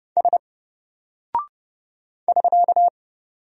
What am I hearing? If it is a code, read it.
Morse code, heard as a keyed tone, about 30 wpm, 700 Hz. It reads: S <SK>
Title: Single Character (Letter:Number:Punctuation:Prosign) Alt 30wpm